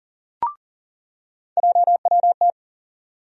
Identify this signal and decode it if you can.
Morse code, heard as a keyed tone, about 37 wpm, 700 Hz. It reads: JWT
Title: USA Call Sign Postfixes - 3 Letters 40wpm